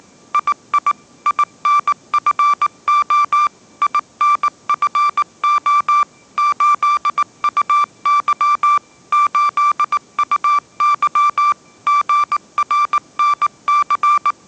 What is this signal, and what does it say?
Morse code, heard as a keyed tone, about 19 wpm, 1170 Hz. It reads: II INFO INFO 8UY 8UY GRNC
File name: mrx083 20wpm